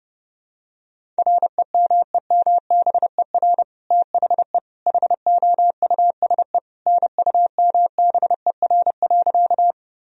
Morse code, heard as a keyed tone, about 30 wpm, 700 Hz. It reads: REMEMBER THE HOUSE NUMBER.
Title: Sentences from Top 1000 Words - Review Part 1 - 30wpm